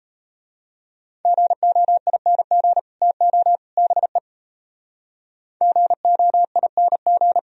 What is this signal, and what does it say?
Morse code, heard as a keyed tone, about 32 wpm, 700 Hz. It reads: GOING TO BE GOING
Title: Speed-Racing - Sets of 3 Words 28wpm